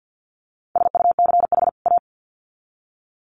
Morse code, heard as a keyed tone, about 50 wpm, 700 Hz. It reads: SUCH A
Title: Sets of 2 Words 50wpm